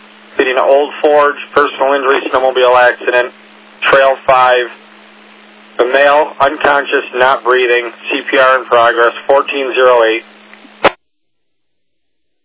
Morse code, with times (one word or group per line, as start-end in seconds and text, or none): none